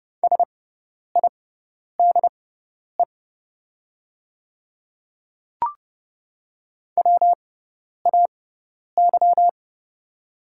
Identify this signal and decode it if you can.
Morse code, heard as a keyed tone, about 30 wpm, spaced as for 10 wpm, 700 Hz. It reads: SIDE WAY
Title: 6x Character Spacing - Top 100 Words - 30wpm